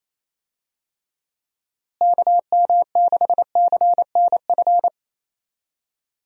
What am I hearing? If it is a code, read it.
Morse code, heard as a keyed tone, about 28 wpm, 700 Hz. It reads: KM6CNF
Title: USA Call Signs - 6 Characters - Repeated 2x - 28wpm